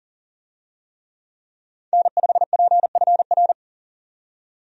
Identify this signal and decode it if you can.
Morse code, heard as a keyed tone, about 38 wpm, 700 Hz. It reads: N5PFR
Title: USA Call Signs - 5 Characters - 40wpm